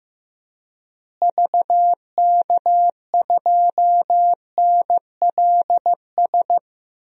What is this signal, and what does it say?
Morse code, heard as a keyed tone, about 14 wpm, 700 Hz. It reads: VK2NLS
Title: DXCC-100 Call Signs - 6 Characters - Repeated 3x - 15wpm